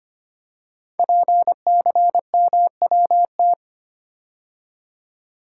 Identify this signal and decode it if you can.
Morse code, heard as a keyed tone, about 25 wpm, 700 Hz. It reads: PCMWT
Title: Code Groups - 5 Letters - 25wpm